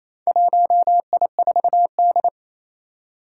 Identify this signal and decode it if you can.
Morse code, heard as a keyed tone, about 27 wpm, 700 Hz. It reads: 1I4D
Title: Four Challenging Letter-Number Combinations 28wpm